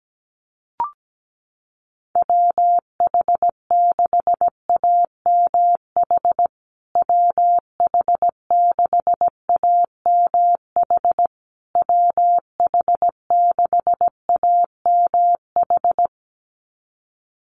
Morse code, heard as a keyed tone, about 17 wpm, 700 Hz. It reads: WH6AMH WH6AMH WH6AMH
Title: USA Call Signs Repeated 3x 17wpm